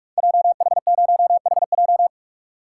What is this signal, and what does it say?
Morse code, heard as a keyed tone, about 44 wpm, 700 Hz. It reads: JH0HJ
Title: Code Groups - 5 Letter:Numbers - 45wpm